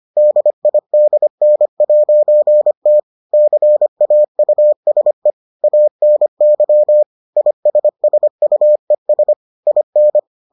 Morse code, heard as a keyed tone, about 25 wpm, 600 Hz. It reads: DIDN'T CAUSE ANY ISSUES IN